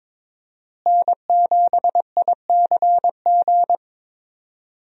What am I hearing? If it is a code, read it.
Morse code, heard as a keyed tone, about 21 wpm, 700 Hz. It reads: N7ICG
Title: USA Call Signs - 5 Characters - Repeated 2x - 22wpm